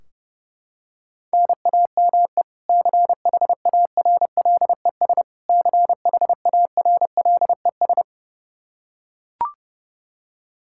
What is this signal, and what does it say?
Morse code, heard as a keyed tone, about 30 wpm, 700 Hz. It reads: NAME CHARLES CHARLES
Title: Practice QSO - Exchange 1 - Beginning - 30wpm